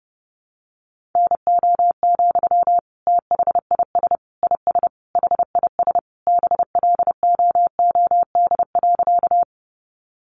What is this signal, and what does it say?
Morse code, heard as a keyed tone, about 30 wpm, 700 Hz. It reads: NO, THIS IS HIS BLOOD.
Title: Sentences from Top 1100 Words 30wpm